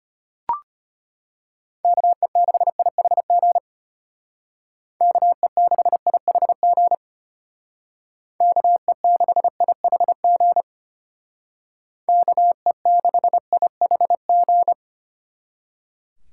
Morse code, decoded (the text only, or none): KE6IHG KE6IHG KE6IHG KE6IHG